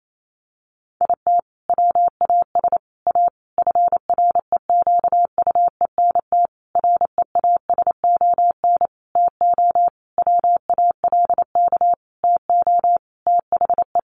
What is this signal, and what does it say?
Morse code, heard as a keyed tone, about 28 wpm, 700 Hz. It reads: IT WAS A FREQUENT REASON TO WALK TO THE
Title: Sentences from Top 1400 Words 28wpm